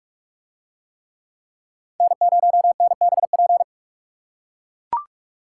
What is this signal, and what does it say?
Morse code, heard as a keyed tone, about 42 wpm, 700 Hz. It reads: N0NBP